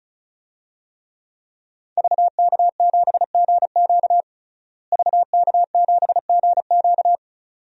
Morse code, heard as a keyed tone, about 36 wpm, 700 Hz. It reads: VK7GQ VK7GQ
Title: DXCC-100 Call Signs - 5 Characters - Repeated 2x - 35wpm